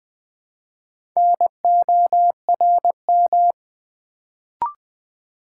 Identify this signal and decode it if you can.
Morse code, heard as a keyed tone, about 20 wpm, 700 Hz. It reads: NORM
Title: Build A Word - 1 Letter At A Time - 3 to 5-Letter Words - 20wpm